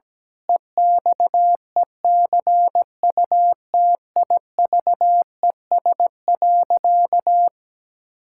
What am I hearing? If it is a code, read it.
Morse code, heard as a keyed tone, about 17 wpm, 700 Hz. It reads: EXECUTIVES.
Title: Pangram Challenge v2 - 17wpm